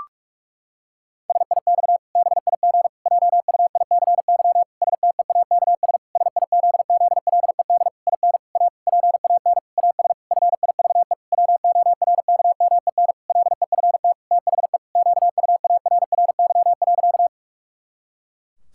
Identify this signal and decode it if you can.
Morse code, heard as a keyed tone, about 41 wpm, 700 Hz. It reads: SIX BIG JUICY STEAKS SIZZLED IN A PAN AS FIVE WORKMEN LEFT THE QUARRY.
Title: Pangram Challenge v2 - 45wpm